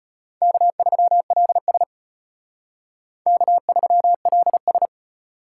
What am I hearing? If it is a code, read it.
Morse code, heard as a keyed tone, about 32 wpm, 700 Hz. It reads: K3LS K3LS